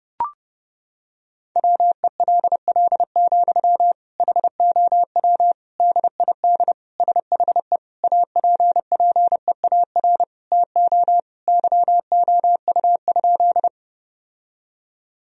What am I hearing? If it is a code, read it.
Morse code, heard as a keyed tone, about 30 wpm, 700 Hz. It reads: WELL, HOW DID SHE APPEAR TO YOU?
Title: Sentences from Top 1000 Words - Review Part 2 - 30wpm